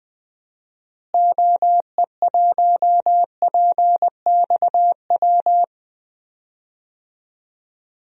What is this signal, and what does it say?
Morse code, heard as a keyed tone, about 20 wpm, 700 Hz. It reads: OE1PXW
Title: DXCC-100 Call Signs - 6 Characters - Repeated 2x - 20wpm